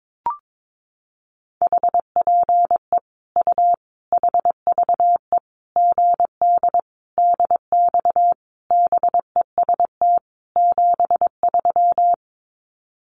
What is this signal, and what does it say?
Morse code, heard as a keyed tone, about 22 wpm, 700 Hz. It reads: HPE U HVE GD DX BEST 73
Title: Common QSO Phrases 22wpm